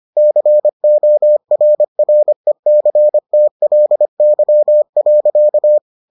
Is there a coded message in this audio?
Yes: Morse code from 0.2 to 5.8 s, heard as a keyed tone, about 25 wpm, 600 Hz. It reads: CORRECTLY.